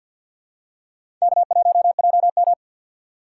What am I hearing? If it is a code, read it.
Morse code, heard as a keyed tone, about 48 wpm, 700 Hz. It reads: K1JM
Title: USA Call Signs - 6x Character Spacing - 50wpm